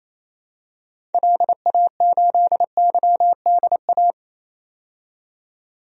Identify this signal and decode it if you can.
Morse code, heard as a keyed tone, about 28 wpm, 700 Hz. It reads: LA8YDA
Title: DXCC-100 Call Signs - 6 Characters - 28wpm